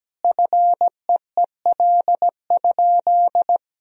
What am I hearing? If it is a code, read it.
Morse code, heard as a keyed tone, about 17 wpm, 700 Hz. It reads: FEEL?
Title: Sentences from Top 400 Words and Abbreviations 17wpm